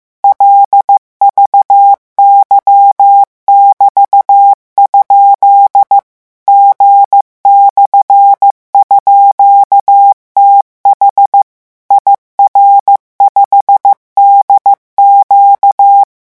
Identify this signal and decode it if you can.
Morse code, heard as a keyed tone, about 15 wpm, 800 Hz. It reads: LVY<BT>? G/_TH IR5DQ
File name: morse mp3 / revision13-vitesse 14 mots